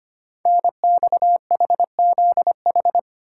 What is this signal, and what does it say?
Morse code, heard as a keyed tone, about 25 wpm, 690 Hz. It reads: NXHZH
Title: Code Groups - 5 Letters - 25wpm